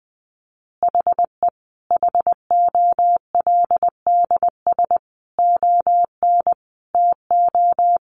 Morse code, heard as a keyed tone, about 20 wpm, 700 Hz. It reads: HE HOLDS ON TO
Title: Additional Easy Phrases 20wpm